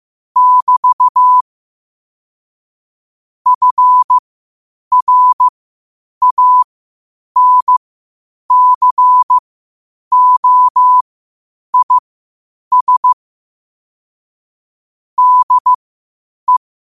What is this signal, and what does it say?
Morse code, heard as a keyed tone, about 15 wpm, spaced as for 8 wpm, 1000 Hz. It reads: <BT> FRANCOIS DE
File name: qotdfiles / 8wpm